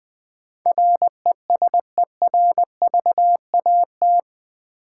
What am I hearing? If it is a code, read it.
Morse code, heard as a keyed tone, about 20 wpm, 700 Hz. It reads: RESERVAT
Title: Build A Word - 4 Letters At A Time - 11-Letter Words - 20wpm